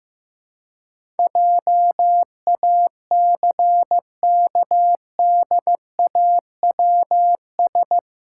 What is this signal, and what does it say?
Morse code, heard as a keyed tone, about 15 wpm, 700 Hz. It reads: JACKDAWS
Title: Pangram Challenge v2 - 15wpm